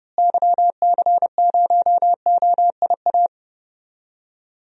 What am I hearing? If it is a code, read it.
Morse code, heard as a keyed tone, about 30 wpm, 700 Hz. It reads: YC0OIA
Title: DXCC-100 Call Signs - 6 Characters - Repeated 2x - 30wpm